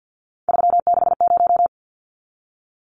Morse code, heard as a keyed tone, about 50 wpm, 700 Hz. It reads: 360
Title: Numbers - 3-Digit 50wpm